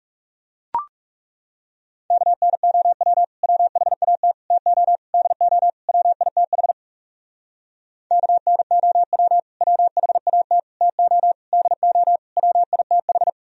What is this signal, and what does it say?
Morse code, heard as a keyed tone, about 42 wpm, 700 Hz. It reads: KNOW WHAT TO DO WITH KNOW WHAT TO DO WITH